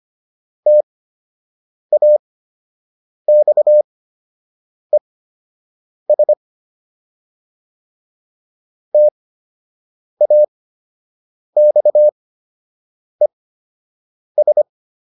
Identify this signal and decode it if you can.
Morse code, heard as a keyed tone, about 25 wpm, spaced as for 7 wpm, 600 Hz. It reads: TAXES TAXES